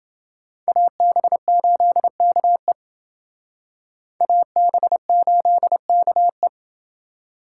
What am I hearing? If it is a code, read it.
Morse code, heard as a keyed tone, about 28 wpm, 700 Hz. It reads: AB8KE AB8KE